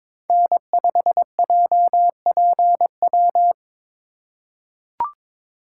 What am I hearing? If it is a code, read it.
Morse code, heard as a keyed tone, about 21 wpm, 700 Hz. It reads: N5JPW
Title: USA Call Signs - 5 Characters - 4x Character Spacing - 22wpm